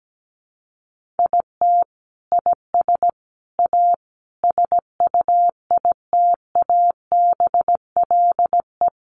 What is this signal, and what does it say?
Morse code, heard as a keyed tone, about 17 wpm, 700 Hz. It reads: IT IS A SUITABLE
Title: Sentences from Top 1400 Words 17wpm